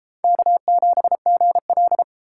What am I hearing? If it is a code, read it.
Morse code, heard as a keyed tone, about 31 wpm, 700 Hz. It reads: K7GL